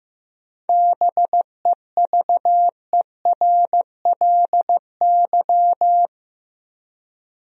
Morse code, heard as a keyed tone, about 15 wpm, 700 Hz. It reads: BEVERLY
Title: USA Common Names - 15wpm